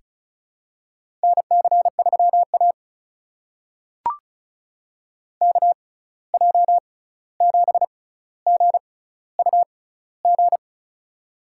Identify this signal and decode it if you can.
Morse code, heard as a keyed tone, about 36 wpm, spaced as for 16 wpm, 700 Hz. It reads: NC3A KJ7GUG
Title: USA Call Signs - 6x Character Spacing - 35wpm